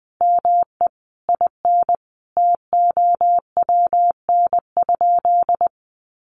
Morse code, heard as a keyed tone, about 20 wpm, 700 Hz. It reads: ME IN TOWN?
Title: Sentences from Top 500 Words 20wpm